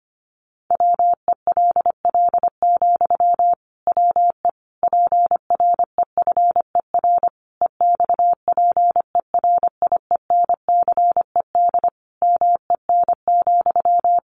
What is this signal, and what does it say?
Morse code, heard as a keyed tone, about 25 wpm, 700 Hz. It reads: WELL, WE PREFER EXPERIENCED MEN,
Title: Sentences from Top 1100 Words 25wpm